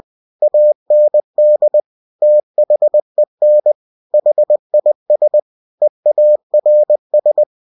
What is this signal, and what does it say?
Morse code, heard as a keyed tone, about 20 wpm, 600 Hz. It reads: AND THEN HIS EARS
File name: cw-books / winnie-the-pooh / winnie-20wpm-page-0016